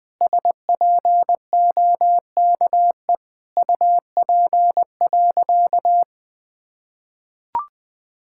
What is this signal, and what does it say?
Morse code, heard as a keyed tone, about 20 wpm, 700 Hz. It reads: SPOKE UP.
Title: Sentences with 6-Word Phrases - 20wpm